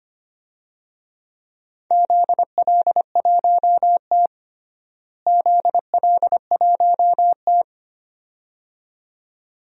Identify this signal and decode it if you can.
Morse code, heard as a keyed tone, about 25 wpm, 700 Hz. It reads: ZL1T ZL1T
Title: DXCC-100 Call Signs - 4 Characters - Repeated 2x - 25wpm